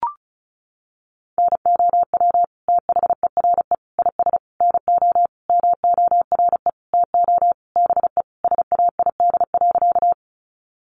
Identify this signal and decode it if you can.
Morse code, heard as a keyed tone, about 34 wpm, 700 Hz. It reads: NOW THERE IS NO MORE TO BE SAID.
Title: Sentences from Top 100 Words 35wpm